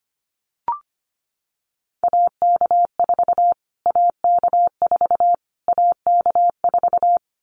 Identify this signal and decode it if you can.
Morse code, heard as a keyed tone, about 25 wpm, 700 Hz. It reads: AK4 AK4 AK4
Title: Callsign Prefixes for Top 25 DXCC - Repeated 3x - 25wpm